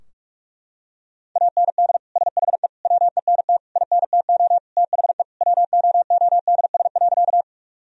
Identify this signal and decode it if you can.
Morse code, heard as a keyed tone, about 41 wpm, 700 Hz. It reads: AND SHE WENT INTO THE WOODS.
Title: Sentences from Top 1000 Words - Review Part 2 - 45wpm